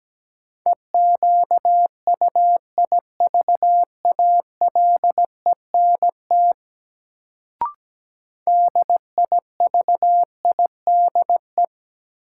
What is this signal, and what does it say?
Morse code, heard as a keyed tone, about 17 wpm, 700 Hz. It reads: EQUIVALENT DIVIDE
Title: Sentences from Top 1600 Words - New Word Drill - 17wpm